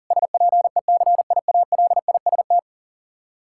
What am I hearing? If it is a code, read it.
Morse code, heard as a keyed tone, about 37 wpm, 700 Hz. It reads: SPECIALIST
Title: Practice QSO - Exchange 3 - More Info - 40wpm